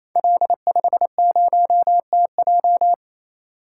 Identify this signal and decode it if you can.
Morse code, heard as a keyed tone, about 28 wpm, 700 Hz. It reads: L50TJ
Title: Code Groups - 5 Letter:Numbers - 28wpm